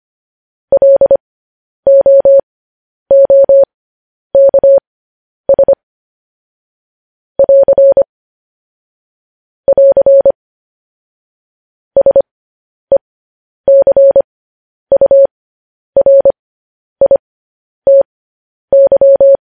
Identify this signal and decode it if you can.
Morse code, heard as a keyed tone, about 25 wpm, spaced as for 10 wpm, 550 Hz. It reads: LOOKS <AR> <AR> SECURITY